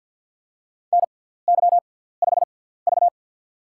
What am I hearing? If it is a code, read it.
Morse code, heard as a keyed tone, about 48 wpm, spaced as for 17 wpm, 700 Hz. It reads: NY5V
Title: USA Call Signs - 4 Characters - 6x Character Spacing - 50wpm